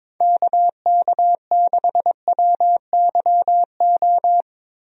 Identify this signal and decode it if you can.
Morse code, heard as a keyed tone, about 22 wpm, 700 Hz. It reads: KK6WYO
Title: USA Call Signs - 6 Characters - Repeated 2x - 22wpm